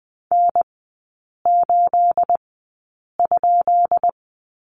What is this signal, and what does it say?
Morse code, heard as a keyed tone, about 20 wpm, spaced as for 8 wpm, 700 Hz. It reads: N8?
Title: 212 Lesson 34e - Number 8 - Three Characters 20wpm